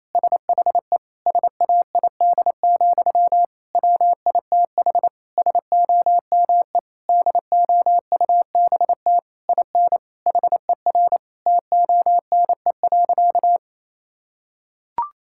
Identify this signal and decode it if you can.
Morse code, heard as a keyed tone, about 28 wpm, 700 Hz. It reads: SHE SAID, WITH SOME DOUBT IN HER TONE.
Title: Sentences from Top 1600 Words 28wpm